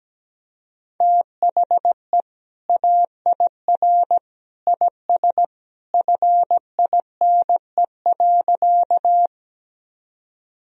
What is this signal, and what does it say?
Morse code, heard as a keyed tone, about 17 wpm, 700 Hz. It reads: THE AIR IS FINE.
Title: Sentences from Top 1000 Words - Review Part 1 - 17wpm